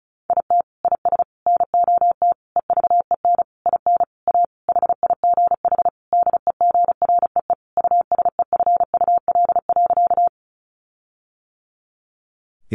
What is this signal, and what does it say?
Morse code, heard as a keyed tone, about 34 wpm, 700 Hz. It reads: IT IS NOT EVEN IN A HIGH DEGREE USEFUL.
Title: Sentences from Top 1500 Words 35wpm